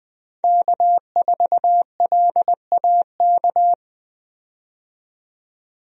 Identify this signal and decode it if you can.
Morse code, heard as a keyed tone, about 20 wpm, 700 Hz. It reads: K4LAK
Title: USA Call Signs - 5 Characters - 20wpm